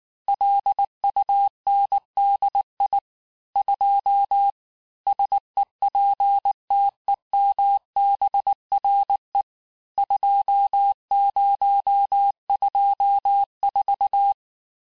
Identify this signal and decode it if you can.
Morse code, heard as a keyed tone, about 19 wpm, 800 Hz. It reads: LUNDI 2 SEPTEMBRE 2024
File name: cours-du-lundi files / cours-1595-teste-CHIFFRE-vitesse18-MPM